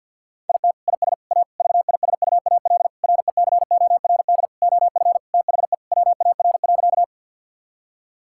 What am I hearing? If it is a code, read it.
Morse code, heard as a keyed tone, about 51 wpm, 700 Hz. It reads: IT IS A VISUAL RECORD OF THE WAR.
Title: Sentences from Top 1000 Words - Review Part 6 - 50wpm